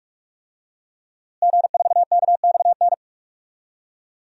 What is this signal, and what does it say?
Morse code, heard as a keyed tone, about 46 wpm, 700 Hz. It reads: G4KXN
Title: DXCC-100 Call Signs - 5 Characters - Repeated 2x - 45wpm